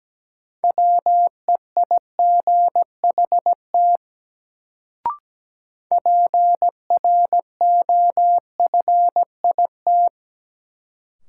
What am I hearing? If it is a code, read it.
Morse code, heard as a keyed tone, about 17 wpm, 700 Hz. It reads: WEIGHT PROFIT
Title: Sentences from Top 800 Words - New Word Drill - 17wpm